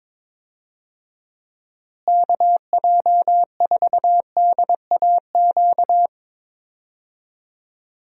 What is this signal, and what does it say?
Morse code, heard as a keyed tone, about 22 wpm, 700 Hz. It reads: KJ4DAQ
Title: USA Call Signs - 6 Characters - Repeated 3x - 22wpm